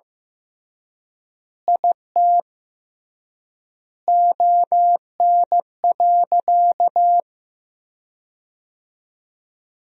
Morse code, heard as a keyed tone, about 15 wpm, 700 Hz. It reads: IT ON.
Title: 3x Word-Spacing - Sentences from Top 200 Words - 15wpm